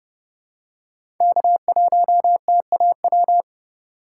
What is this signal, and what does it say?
Morse code, heard as a keyed tone, about 31 wpm, 700 Hz. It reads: K1TAW
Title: USA Call Signs - 5 Characters - 3x Character Spacing - 30wpm